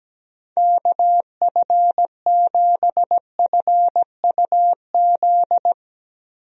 Morse code, heard as a keyed tone, about 17 wpm, 700 Hz. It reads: KF7FUZ
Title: USA Call Signs - 6 Characters - 17wpm